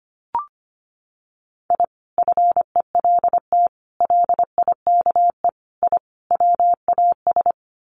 Morse code, heard as a keyed tone, about 25 wpm, 690 Hz. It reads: I FELT LIKE I WAS
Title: Sets of 5 Words Encore 25wpm